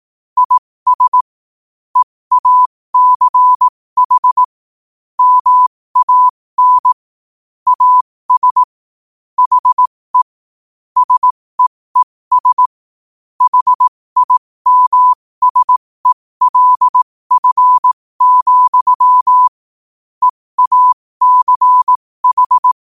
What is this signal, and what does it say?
Morse code, heard as a keyed tone, about 18 wpm, spaced as for 15 wpm, 1000 Hz. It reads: IS EACH MAN AS HE SEES HIMSELF, EACH
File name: qotdfiles / 15wpm